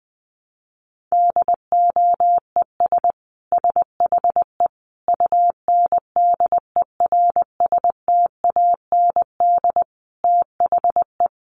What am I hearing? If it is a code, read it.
Morse code, heard as a keyed tone, about 19 wpm, 700 Hz. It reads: DOES SHE UNDERSTAND THE
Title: Sentences from Top 1100 Words 20wpm